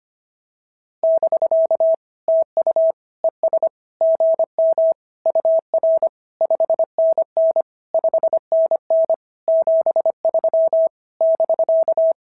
Morse code, heard as a keyed tone, about 24 wpm, 650 Hz. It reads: <BK> TU ES GM UR 5NN 5NN 73 <BK>
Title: SOTA QSO Practice 25wpm